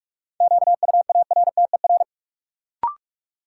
Morse code, heard as a keyed tone, about 46 wpm, 700 Hz. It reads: QUARTER
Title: Sentences from Top 1300 Words - New Word Drill - 45wpm